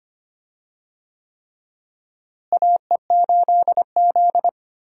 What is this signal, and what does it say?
Morse code, heard as a keyed tone, about 25 wpm, 700 Hz. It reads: AE8Z
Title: USA Call Signs - 4 Characters - Repeated 3x - 25wpm